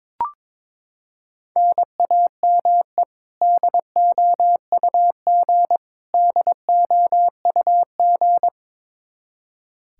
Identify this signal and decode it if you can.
Morse code, heard as a keyed tone, about 22 wpm, 700 Hz. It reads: NAME DOUG DOUG
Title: Practice QSO - Exchange 1 - Beginning - 22wpm